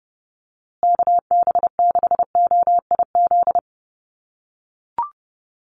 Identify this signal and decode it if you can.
Morse code, heard as a keyed tone, about 30 wpm, 700 Hz. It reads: KB6OIZ
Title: USA Call Signs Repeated 2x 30wpm